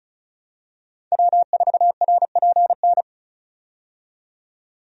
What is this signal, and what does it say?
Morse code, heard as a keyed tone, about 36 wpm, 700 Hz. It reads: W4RPN
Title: USA Call Signs - 5 Characters - 35wpm